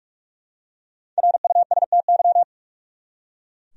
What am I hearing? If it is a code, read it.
Morse code, heard as a keyed tone, about 44 wpm, 700 Hz. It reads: RUSTY
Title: USA Common Names - Max 5 Letters - 45wpm